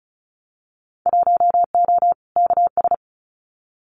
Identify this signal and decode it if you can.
Morse code, heard as a keyed tone, about 36 wpm, 700 Hz. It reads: 1O KS
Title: Winter Field Day Exchanges 35wpm